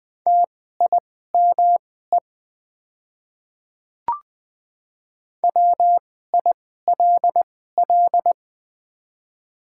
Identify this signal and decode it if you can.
Morse code, heard as a keyed tone, about 20 wpm, spaced as for 14 wpm, 700 Hz. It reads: TIME WILL
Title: 2x Character Spacing - Top 100 Words - 20wpm